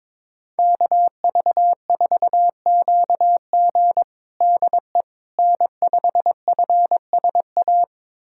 Morse code, heard as a keyed tone, about 22 wpm, 700 Hz. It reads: KV4QG DE N5FSA
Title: USA Callsign DE Callsign - 5 Characters - 22wpm